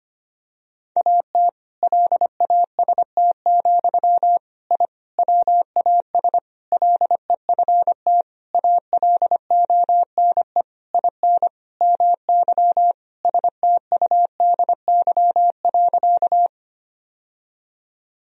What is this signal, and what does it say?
Morse code, heard as a keyed tone, about 25 wpm, 700 Hz. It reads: AT LAST, I WAS LEFT ALONE IN MY STUDY.
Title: Sentences from Top 1000 Words - Review Part 1 - 25wpm